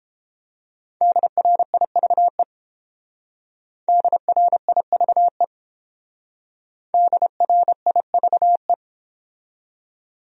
Morse code, heard as a keyed tone, about 29 wpm, 700 Hz. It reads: DRIVE DRIVE DRIVE